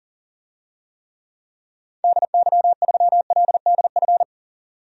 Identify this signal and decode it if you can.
Morse code, heard as a keyed tone, about 39 wpm, 700 Hz. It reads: DY3LDF
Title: DXCC-100 Call Signs - 6 Characters - 40wpm